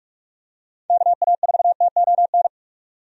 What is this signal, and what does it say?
Morse code, heard as a keyed tone, about 46 wpm, 700 Hz. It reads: KA4TON
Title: USA Call Signs - 6 Characters - 45wpm